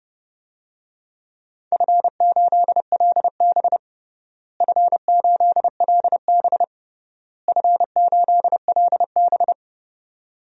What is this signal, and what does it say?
Morse code, heard as a keyed tone, about 30 wpm, 700 Hz. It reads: F8LB F8LB F8LB
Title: DXCC-100 Call Signs - 4 Characters - Repeated 3x - 30wpm